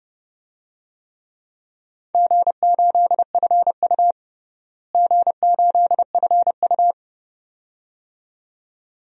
Morse code, heard as a keyed tone, about 30 wpm, 700 Hz. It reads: G8FU G8FU
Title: DXCC-100 Call Signs - 4 Characters - Repeated 2x - 30wpm